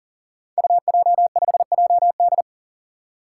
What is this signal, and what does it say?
Morse code, heard as a keyed tone, about 38 wpm, 690 Hz. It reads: UJ5JD
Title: Code Groups - 5 Letter:Numbers - 40wpm